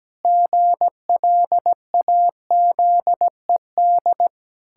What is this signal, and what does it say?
Morse code, heard as a keyed tone, about 17 wpm, 700 Hz. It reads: GLAZED
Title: Pangram Challenge v2 - 17wpm